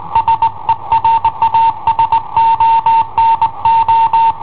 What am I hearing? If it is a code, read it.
Morse code, heard as a keyed tone, about 20 wpm, 910 Hz. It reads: SERA SONO
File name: i7ohp / cassetta i7ohp B